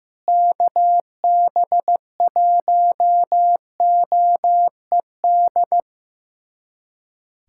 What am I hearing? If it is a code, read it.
Morse code, heard as a keyed tone, about 16 wpm, 700 Hz. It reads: KB1OED
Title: USA Callsign DE Callsign - 6 Characters - 15wpm